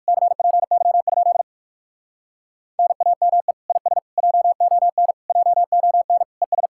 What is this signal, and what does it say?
Morse code, heard as a keyed tone, about 56 wpm, 700 Hz. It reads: CPY? NAME IS JON JON ES
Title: Speed-Racing Common QSO Phrases Encore - 45wpm